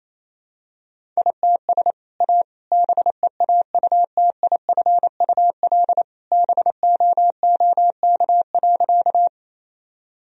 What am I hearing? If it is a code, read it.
Morse code, heard as a keyed tone, about 28 wpm, 700 Hz. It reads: ITS A BEAUTIFUL BOOK.
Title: Sentences from Top 1000 Words - Review Part 1 - 28wpm